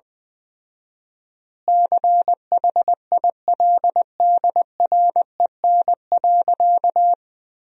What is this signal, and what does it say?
Morse code, heard as a keyed tone, about 20 wpm, 700 Hz. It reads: CHILDREN.
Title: 4x Word-Spacing - Sentences from Top 400 Words - 20wpm